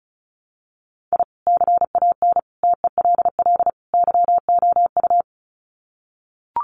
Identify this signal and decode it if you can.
Morse code, heard as a keyed tone, about 34 wpm, 700 Hz. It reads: I CAN TELL YOU
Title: Sets of 4 Words 35wpm